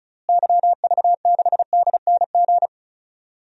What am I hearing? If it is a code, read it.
Morse code, heard as a keyed tone, about 36 wpm, 700 Hz. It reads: YV6DNG
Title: DXCC-100 Call Signs - 6 Characters - Repeated 2x - 35wpm